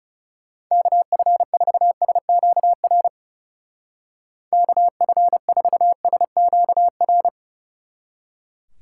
Morse code, heard as a keyed tone, about 37 wpm, 700 Hz. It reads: KF4SQR KF4SQR
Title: Speed Racing - USA 2x3 Call Signs 30wpm